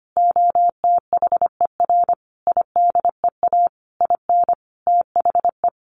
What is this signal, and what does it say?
Morse code, heard as a keyed tone, about 25 wpm, 700 Hz. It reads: OTHER IDEA IN THE
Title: Sentences from Top 1300 Words 25wpm